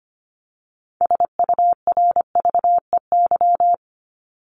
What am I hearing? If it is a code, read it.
Morse code, heard as a keyed tone, about 25 wpm, 700 Hz. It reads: SURVEY
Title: Top 1501-1750 Words 25wpm